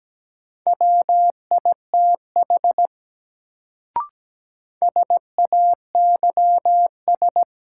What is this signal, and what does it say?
Morse code, heard as a keyed tone, about 17 wpm, 700 Hz. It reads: WITH SAYS
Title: Sets of 2 Words - New Word Drill - 17wpm